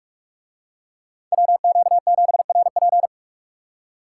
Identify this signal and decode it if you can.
Morse code, heard as a keyed tone, about 44 wpm, 700 Hz. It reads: WQ7RP